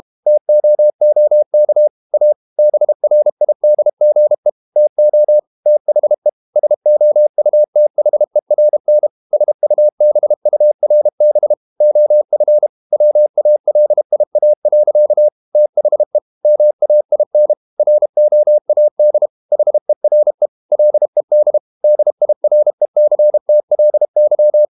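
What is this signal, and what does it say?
Morse code, heard as a keyed tone, about 32 wpm, 600 Hz. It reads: TOOK A BRIDGE TO THE SOUTHERN SUBURB OF WALIA. THE MAIN ROAD HERE LED DIRECTLY
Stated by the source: Morse code practice at 32 wpm from CW med Gnister, reading Ken Follett's novel Never in English.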